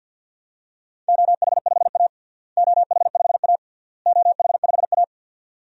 Morse code, heard as a keyed tone, about 53 wpm, 700 Hz. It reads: OH5A OH5A OH5A
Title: DXCC-100 Call Signs - 4 Characters - Repeated 3x - 50wpm